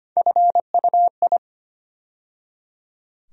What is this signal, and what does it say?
Morse code, heard as a keyed tone, about 25 wpm, 700 Hz. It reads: FUI
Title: USA Call Sign Postfixes - 3 Letters 25wpm